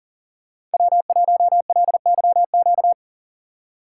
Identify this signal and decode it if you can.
Morse code, heard as a keyed tone, about 40 wpm, 700 Hz. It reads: W1LYQ
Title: USA Call Signs - 5 Characters - 2x Character Spacing - 40wpm